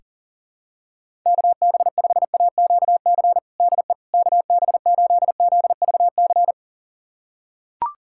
Morse code, heard as a keyed tone, about 40 wpm, 700 Hz. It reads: KB5AQC DE KB8ZVC
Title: USA Callsign DE Callsign - 6 Characters - 40wpm